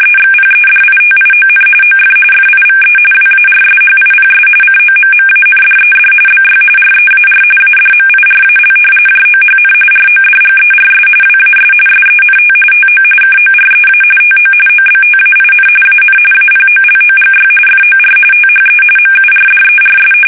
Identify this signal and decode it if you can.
RTTY, about 45.45 baud. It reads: EXPEDITION OF FIVE WARSHIPS COMMANDED BY REAR ADMIRAL JOHN RODGERS WAS SENT TO KOREA TO OBTAIN AN APOLOGY FOR THE MURDERS
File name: fsk850 / usnavy3